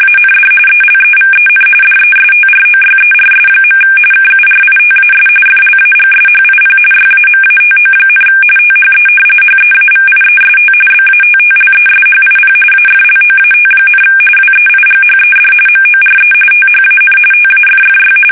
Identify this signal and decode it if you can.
RTTY, about 45.45 baud. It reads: AND ELEVEN DESTROYERS. KURITAS FLAGSHIP YAMATO, THE HEAVIEST BATTLESHIP AFLOAT, WEIGHED MORE THAN TAFFY
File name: fsk850 / battleshipnj